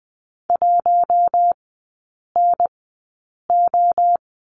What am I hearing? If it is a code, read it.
Morse code, heard as a keyed tone, about 20 wpm, spaced as for 8 wpm, 700 Hz. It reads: 1NO
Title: 047 Lesson 9e - Number 4 - Three Characters 20wpm